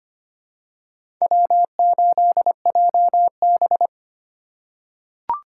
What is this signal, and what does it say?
Morse code, heard as a keyed tone, about 25 wpm, 700 Hz. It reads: W8JB
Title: USA Call Signs - 4 Characters - 6x Character Spacing - 25wpm